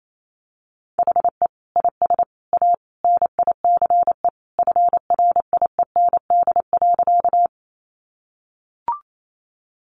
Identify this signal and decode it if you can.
Morse code, heard as a keyed tone, about 28 wpm, 700 Hz. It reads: HE IS A NICE FRIEND.
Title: Sentences from Top 1100 Words 28wpm